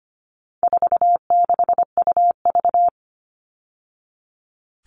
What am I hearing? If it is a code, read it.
Morse code, heard as a keyed tone, about 25 wpm, 700 Hz. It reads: 46UV
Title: All Cut Numbers - 4-Digit 25wpm